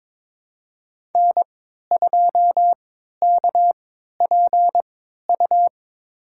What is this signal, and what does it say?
Morse code, heard as a keyed tone, about 21 wpm, spaced as for 12 wpm, 700 Hz. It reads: N2KPU
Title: USA Call Signs - 5 Characters - 3x Character Spacing - 22wpm